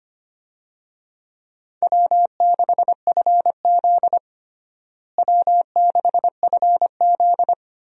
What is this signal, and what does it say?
Morse code, heard as a keyed tone, about 25 wpm, 700 Hz. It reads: W6FZ W6FZ
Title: USA Call Signs - 4 Characters - Repeated 2x - 25wpm